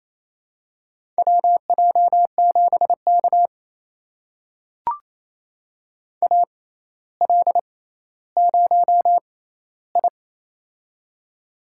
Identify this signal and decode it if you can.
Morse code, heard as a keyed tone, about 28 wpm, spaced as for 14 wpm, 700 Hz. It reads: WJ7K AL0I
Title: USA Call Signs - 4 Characters - 6x Character Spacing - 28wpm